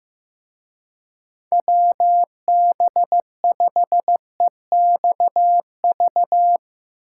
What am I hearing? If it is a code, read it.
Morse code, heard as a keyed tone, about 16 wpm, 700 Hz. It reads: WB5EXV
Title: Speed Racing - USA 2x3 Call Signs 15wpm